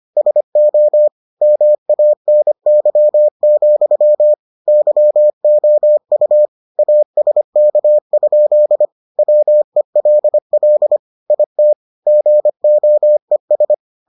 Morse code, heard as a keyed tone, about 25 wpm, 600 Hz. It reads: SO MANY, YOU ASK? WELL IT GOES